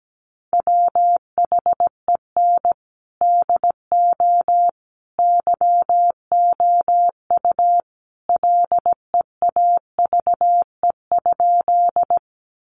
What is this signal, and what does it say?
Morse code, heard as a keyed tone, about 17 wpm, 700 Hz. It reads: WHEN DO YOU LEAVE?
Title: Sentences from Top 400 Words 17wpm